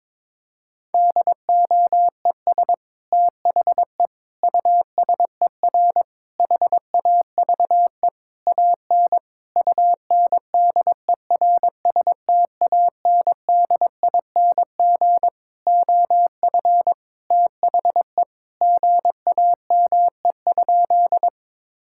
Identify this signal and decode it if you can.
Morse code, heard as a keyed tone, about 22 wpm, 700 Hz. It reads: DOES THE USER HAVE AN UNDERSTANDING OF THE GAME?
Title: Sentences from Top 1000 Words - Review Part 5 - 22wpm